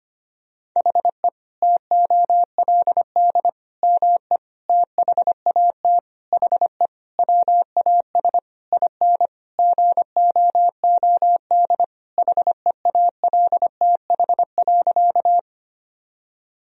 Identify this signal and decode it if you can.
Morse code, heard as a keyed tone, about 25 wpm, 700 Hz. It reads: HE TOLD ME THAT HE WAS IN GOOD HEALTH.
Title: Sentences from Top 1000 Words - Review Part 3 - 25wpm